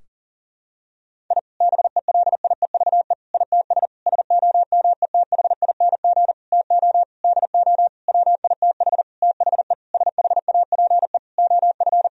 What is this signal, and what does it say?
Morse code, heard as a keyed tone, about 40 wpm, 700 Hz. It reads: I BELIEVE ITS SOMETHING TO DO WITH THE SHAPE OF
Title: Sentences from Top 1000 Words - Review Part 3 - 40wpm